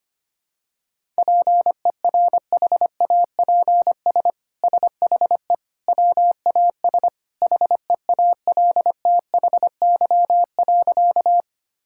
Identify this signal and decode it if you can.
Morse code, heard as a keyed tone, about 25 wpm, 700 Hz. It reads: PERHAPS SHE WAS HEALTHY.
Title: Sentences from Top 1000 Words - Review Part 5 - 25wpm